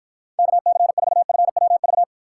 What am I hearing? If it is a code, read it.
Morse code, heard as a keyed tone, about 53 wpm, 700 Hz. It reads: KC4FWV